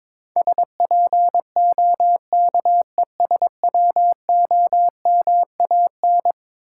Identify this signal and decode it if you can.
Morse code, heard as a keyed tone, about 23 wpm, 700 Hz. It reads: SPOKESWOMAN
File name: practice / 22wpm / Eleven Letter Words - 22wpm